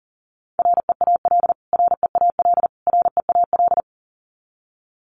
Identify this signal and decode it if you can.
Morse code, heard as a keyed tone, about 40 wpm, 700 Hz. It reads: REAL REAL REAL
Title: Top 500 Words Repeated 40wpm